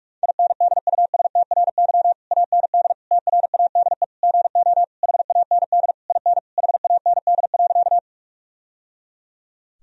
Morse code, heard as a keyed tone, about 46 wpm, 700 Hz. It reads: INDUSTRY AND TRADE GO HAND IN HAND.
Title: Sentences from Top 1000 Words - Review Part 4 - 45wpm